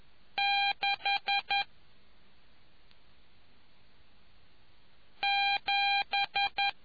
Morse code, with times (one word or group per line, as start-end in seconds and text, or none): 0.4-1.6 6
5.2-6.7 7